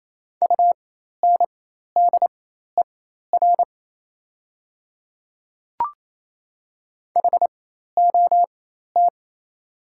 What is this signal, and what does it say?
Morse code, heard as a keyed tone, about 28 wpm, spaced as for 13 wpm, 700 Hz. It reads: UNDER HOT
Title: 4x Character Spacing - Top 500 Words - 28wpm